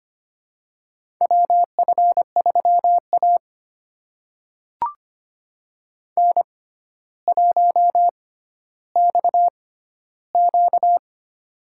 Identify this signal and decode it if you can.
Morse code, heard as a keyed tone, about 25 wpm, spaced as for 13 wpm, 700 Hz. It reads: WF3A N1XQ
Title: USA Call Signs - 4 Characters - 6x Character Spacing - 25wpm